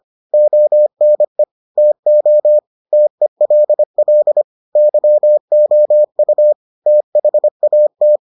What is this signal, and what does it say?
Morse code, heard as a keyed tone, about 25 wpm, 600 Hz. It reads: ONE TO TELL YOU THAT